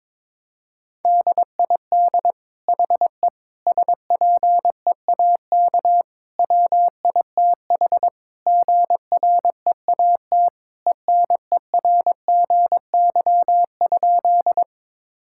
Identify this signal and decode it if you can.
Morse code, heard as a keyed tone, about 21 wpm, 700 Hz. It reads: DID HE SPEAK WITH GREAT ENERGY?
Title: Sentences from Top 1000 Words - Review Part 6 - 22wpm